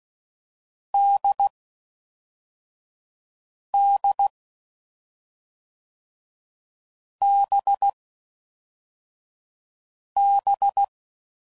Morse code, heard as a keyed tone, about 16 wpm, 800 Hz. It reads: D D B B
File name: Mixed / Short Runs / awj1ndb6gz7